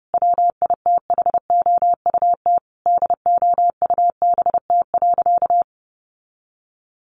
Morse code, heard as a keyed tone, about 28 wpm, 700 Hz. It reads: WITHOUT DOUBT.
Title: Sentences from Top 1500 Words 30wpm